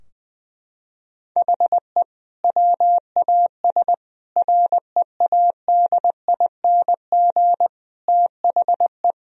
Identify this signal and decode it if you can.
Morse code, heard as a keyed tone, about 20 wpm, 700 Hz. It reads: HE WAS READING THE
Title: Sentences from Top 1000 Words - Review Part 3 - 20wpm